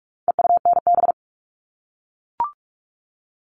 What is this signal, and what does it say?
Morse code, heard as a keyed tone, about 44 wpm, 700 Hz. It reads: EUNB
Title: All Cut Numbers - 4-Digit 45wpm